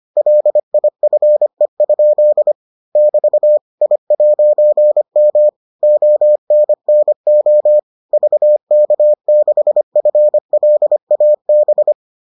Morse code, heard as a keyed tone, about 25 wpm, 600 Hz. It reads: LIFE? <BT> I'M ONNO VK6FLAB